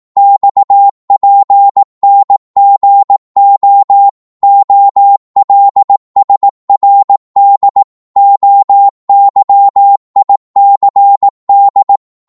Morse code, heard as a keyed tone, about 18 wpm, 800 Hz. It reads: XPNGO OLSRD OYICD
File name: Morse / ltr-18wpm